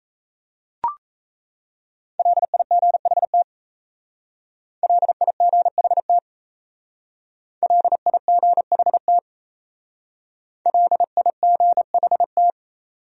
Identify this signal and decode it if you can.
Morse code, heard as a keyed tone, about 35 wpm, 700 Hz. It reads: LIGHT LIGHT LIGHT LIGHT
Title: Speed-Racing Top 200 Words 28wpm